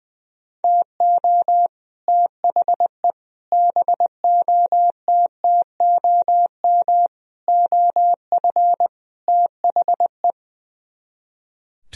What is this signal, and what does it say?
Morse code, heard as a keyed tone, about 21 wpm, 700 Hz. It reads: TO THE BOTTOM OF THE
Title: Speed-Racing - Sets of 5 Words 20wpm